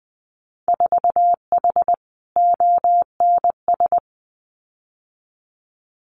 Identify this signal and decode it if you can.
Morse code, heard as a keyed tone, about 20 wpm, 700 Hz. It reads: 4H ONS
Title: Winter Field Day Exchanges Encore 20wpm